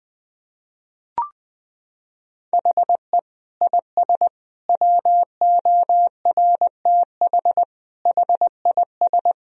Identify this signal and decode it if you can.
Morse code, heard as a keyed tone, about 20 wpm, 700 Hz. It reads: HE IS WORTH HIS
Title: Sentences from Top 1000 Words - Review Part 3 - 20wpm